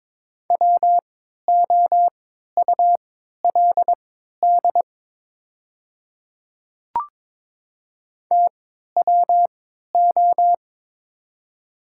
Morse code, heard as a keyed tone, about 22 wpm, spaced as for 13 wpm, 700 Hz. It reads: WOULD TWO
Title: 3x Character Spacing - Top 100 Words - 22wpm